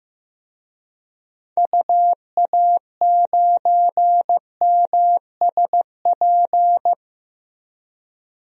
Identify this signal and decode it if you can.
Morse code, heard as a keyed tone, about 15 wpm, 700 Hz. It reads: UA9MSP
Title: DXCC-100 Call Signs - 6 Characters - 15wpm